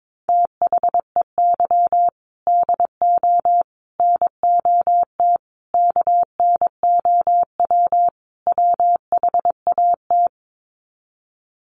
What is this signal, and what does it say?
Morse code, heard as a keyed tone, about 22 wpm, 700 Hz. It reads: THEY DO NOT KNOW WHAT
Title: Sets of 5 Words Encore 22wpm